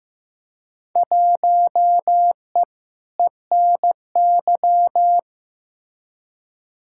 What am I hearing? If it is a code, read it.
Morse code, heard as a keyed tone, about 15 wpm, 700 Hz. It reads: 1E ENY
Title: Field Day Exchanges - 15wpm